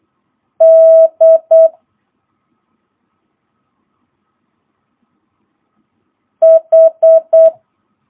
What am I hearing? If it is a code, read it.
Morse code, heard as a keyed tone, about 8 wpm, 650 Hz. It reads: D H
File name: jh8cbh / 2022cw